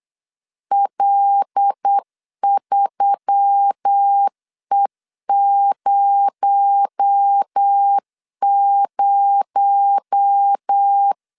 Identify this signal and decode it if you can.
Morse code, heard as a keyed tone, about 8 wpm, 800 Hz. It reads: L3E00
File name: LEZIONI / 6 PKX0 / 6 PKX0 8wpm 3341